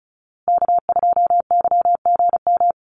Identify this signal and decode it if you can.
Morse code, heard as a keyed tone, about 34 wpm, 700 Hz. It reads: K2YGM
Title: Call Signs Repeated 3x - 35wpm